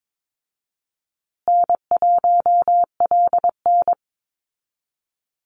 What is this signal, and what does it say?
Morse code, heard as a keyed tone, about 22 wpm, 700 Hz. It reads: N1LN
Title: Call Signs Repeated 2x - 22wpm